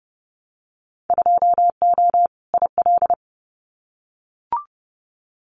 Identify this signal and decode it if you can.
Morse code, heard as a keyed tone, about 30 wpm, 700 Hz. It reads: 2O IL
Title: Winter Field Day Exchanges 30wpm